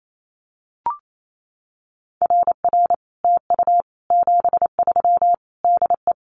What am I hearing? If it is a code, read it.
Morse code, heard as a keyed tone, about 28 wpm, 700 Hz. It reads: RR TU 73 DE
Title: Common QSO Phrases Encore 28wpm